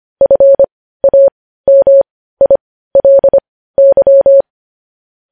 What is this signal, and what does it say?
Morse code, heard as a keyed tone, about 25 wpm, spaced as for 15 wpm, 550 Hz. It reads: FAMILY
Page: News Headlines in Morse Code at 15 WPM